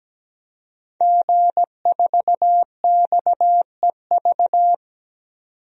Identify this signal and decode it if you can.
Morse code, heard as a keyed tone, about 17 wpm, 700 Hz. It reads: G4XEV
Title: DXCC-100 Call Signs - 5 Characters - 17wpm